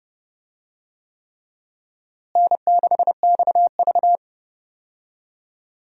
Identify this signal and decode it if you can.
Morse code, heard as a keyed tone, about 30 wpm, 700 Hz. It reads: N6XV
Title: USA Call Signs - 4 Characters - Repeated 3x - 30wpm